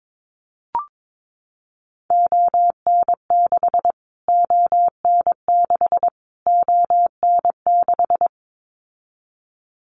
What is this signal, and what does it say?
Morse code, heard as a keyed tone, about 22 wpm, 700 Hz. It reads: ON6 ON6 ON6
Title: Callsign Prefixes for Top 25 DXCC - Repeated 3x - 22wpm